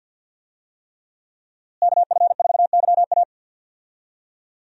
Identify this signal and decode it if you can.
Morse code, heard as a keyed tone, about 46 wpm, 700 Hz. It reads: KF4YA
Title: USA Call Signs - 5 Characters - Repeated 3x - 50wpm